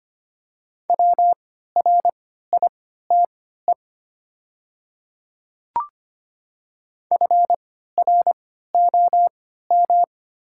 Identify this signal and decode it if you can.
Morse code, heard as a keyed tone, about 25 wpm, spaced as for 14 wpm, 700 Hz. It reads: WRITE FROM
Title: 3x Character Spacing - Top 100 Words - 25wpm